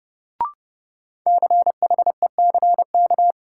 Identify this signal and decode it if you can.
Morse code, heard as a keyed tone, about 31 wpm, 700 Hz. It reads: CHECK
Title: Sentences from Top 1000 Words - Review Part 3 - New Word Drill - 30wpm